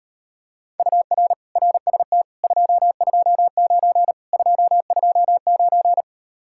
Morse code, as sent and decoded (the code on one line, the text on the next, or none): ..- .-. / .-. ... - / ..--- ..--- ----. / ..--- ..--- ----.
UR RST 229 229